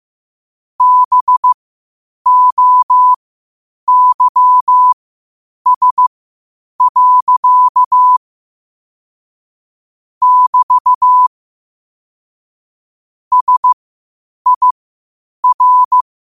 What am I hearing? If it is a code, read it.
Morse code, heard as a keyed tone, about 15 wpm, spaced as for 8 wpm, 1000 Hz. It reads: BOYS. <BT> SIR
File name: qotdfiles / 8wpm